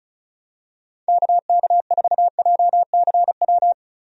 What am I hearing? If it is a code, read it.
Morse code, heard as a keyed tone, about 36 wpm, 700 Hz. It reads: KK4JCW
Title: USA Call Signs - 6 Characters - Repeated 2x - 35wpm